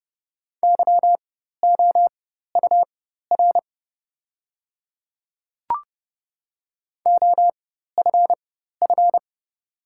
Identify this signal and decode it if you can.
Morse code, heard as a keyed tone, about 30 wpm, spaced as for 14 wpm, 700 Hz. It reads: YOUR OFF
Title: 4x Character Spacing - Top 200 Words - 30wpm